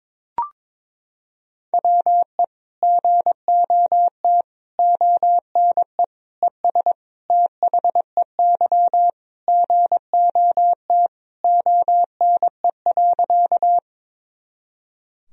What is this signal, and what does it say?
Morse code, heard as a keyed tone, about 22 wpm, 700 Hz. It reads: WE GOT ONE ES THEY GOT ONE.
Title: Sentences from Top 300 Words and Abbreviations 22wpm